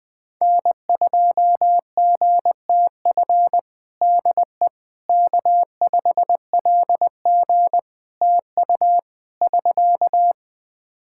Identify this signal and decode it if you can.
Morse code, heard as a keyed tone, about 20 wpm, 700 Hz. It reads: N2GTF DE K5LG TU <SK>
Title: Practice QSO - Exchange 4 - Ending - 20wpm